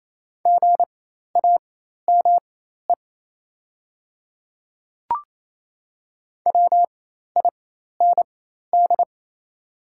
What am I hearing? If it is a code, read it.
Morse code, heard as a keyed tone, about 28 wpm, spaced as for 13 wpm, 700 Hz. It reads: GAME WIND
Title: 4x Character Spacing - Top 500 Words - 28wpm